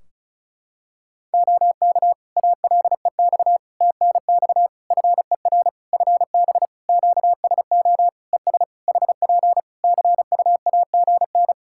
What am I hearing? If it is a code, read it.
Morse code, heard as a keyed tone, about 36 wpm, 700 Hz. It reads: OK ALEX TNX FER FB QSO ES HP CUAGN
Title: Practice QSO - Exchange 4 - Ending - 35wpm